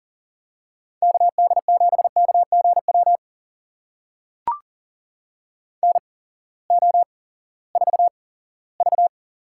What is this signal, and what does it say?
Morse code, heard as a keyed tone, about 40 wpm, spaced as for 20 wpm, 700 Hz. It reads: KD7KGW NO4V
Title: USA Call Signs - 8x Character Spacing - 40wpm